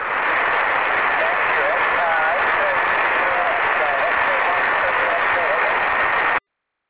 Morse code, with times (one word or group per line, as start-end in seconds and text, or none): none